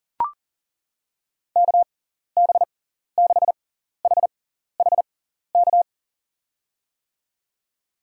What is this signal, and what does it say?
Morse code, heard as a keyed tone, about 40 wpm, spaced as for 14 wpm, 700 Hz. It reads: KB6HHK
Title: USA Call Signs - 6 Characters - 6x Character Spacing - 40wpm